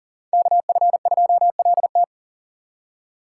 Morse code, heard as a keyed tone, about 52 wpm, 700 Hz. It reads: KF2LT